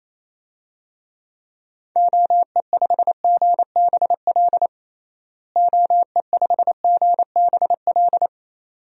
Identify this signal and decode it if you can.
Morse code, heard as a keyed tone, about 27 wpm, 700 Hz. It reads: OE5GBL OE5GBL
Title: DXCC-100 Call Signs - 6 Characters - Repeated 2x - 28wpm